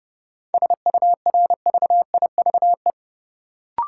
Morse code, heard as a keyed tone, about 30 wpm, 700 Hz. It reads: SURVIVE
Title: Sentences from Top 1300 Words - New Word Drill - 30wpm